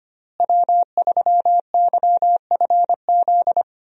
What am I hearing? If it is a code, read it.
Morse code, heard as a keyed tone, about 25 wpm, 700 Hz. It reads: W3YFZ
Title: USA Call Signs - 5 Characters - Repeated 3x - 25wpm